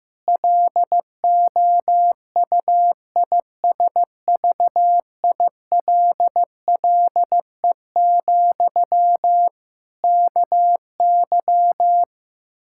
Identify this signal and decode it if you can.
Morse code, heard as a keyed tone, about 15 wpm, 700 Hz. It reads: LOUISVILLE, KY
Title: K1USN SST QSO Practice 15wpm